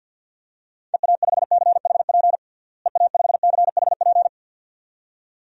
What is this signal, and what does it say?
Morse code, heard as a keyed tone, about 50 wpm, 700 Hz. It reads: EA5CHP EA5CHP
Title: DXCC-100 Call Signs - 6 Characters - Repeated 2x - 50wpm